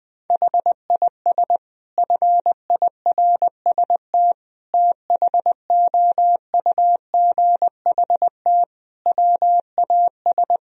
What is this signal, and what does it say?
Morse code, heard as a keyed tone, about 20 wpm, 700 Hz. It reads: HIS FIRST THOUGHT WAS
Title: Sentences from Top 1000 Words - Review Part 3 - 20wpm